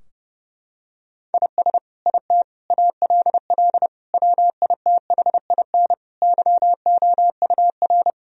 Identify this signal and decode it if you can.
Morse code, heard as a keyed tone, about 32 wpm, 700 Hz. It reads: IS IT ALL WITHIN YOUR
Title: Sentences from Top 1000 Words - Review Part 2 - 30wpm